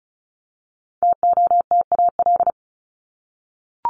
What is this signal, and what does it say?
Morse code, heard as a keyed tone, about 34 wpm, 700 Hz. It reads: TOTAL
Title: Top 1251-1500 Words 35wpm